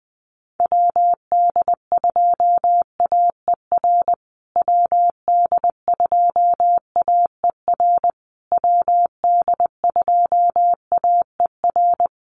Morse code, heard as a keyed tone, about 20 wpm, 700 Hz. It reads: WD2AER WD2AER WD2AER
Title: USA Call Signs Repeated 3x 20wpm